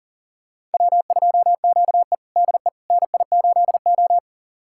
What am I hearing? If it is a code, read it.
Morse code, heard as a keyed tone, about 40 wpm, 700 Hz. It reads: W2QE DE NI8O
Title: USA Callsign DE Callsign - 4 Characters - 40wpm